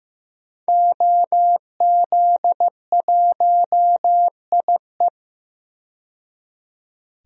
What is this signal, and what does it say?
Morse code, heard as a keyed tone, about 15 wpm, 700 Hz. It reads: OZ1IE
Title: DXCC-100 Call Signs - 5 Characters - 15wpm